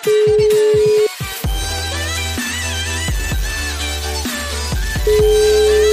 Ringback tone (Europe, 425 Hz), with noise.